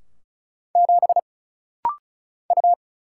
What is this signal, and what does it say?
Morse code, heard as a keyed tone, about 35 wpm, 700 Hz. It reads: 7 U
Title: Single Letter-Number - Rapid-Fire 35wpm